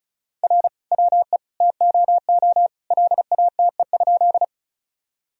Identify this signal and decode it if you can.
Morse code, heard as a keyed tone, about 34 wpm, 700 Hz. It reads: R WE TOO LATE?
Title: Sentences from Top 300 Words and Abbreviations 35wpm